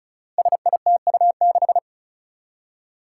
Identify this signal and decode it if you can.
Morse code, heard as a keyed tone, about 36 wpm, 700 Hz. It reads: SITU6
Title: Five Challenging Letter-Number Combinations 35wpm